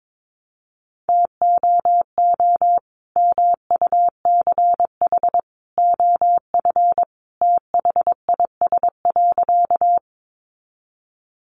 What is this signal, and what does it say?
Morse code, heard as a keyed tone, about 22 wpm, 700 Hz. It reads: TOO MUCH OF THIS.
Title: Sentences from Top 200 Words 22wpm